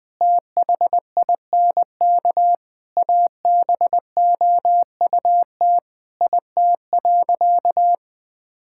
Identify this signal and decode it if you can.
Morse code, heard as a keyed tone, about 19 wpm, 700 Hz. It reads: THINK ABOUT IT.
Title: Sentences with 7-Word Phrases - 20wpm